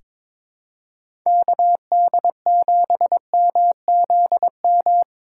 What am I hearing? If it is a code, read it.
Morse code, heard as a keyed tone, about 22 wpm, 700 Hz. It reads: KD7MZM
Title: USA Call Signs - 3x Character Spacing - 22wpm